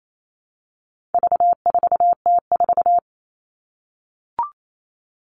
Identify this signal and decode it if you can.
Morse code, heard as a keyed tone, about 28 wpm, 700 Hz. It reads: V4T4
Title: All Cut Numbers - 4-Digit 28wpm